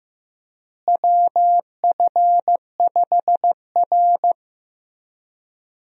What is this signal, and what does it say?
Morse code, heard as a keyed tone, about 15 wpm, 700 Hz. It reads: WF5R
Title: USA Call Signs - 4 Characters - Repeated 3x - 15wpm